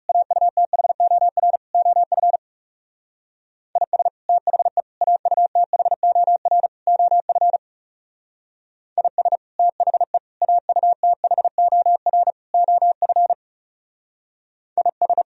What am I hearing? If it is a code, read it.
Morse code, heard as a keyed tone, about 34 wpm, 700 Hz. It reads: AUTHOR OF IS THE AUTHOR OF IS THE AUTHOR OF IS
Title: Speed-Racing - Sets of 4 Words 30wpm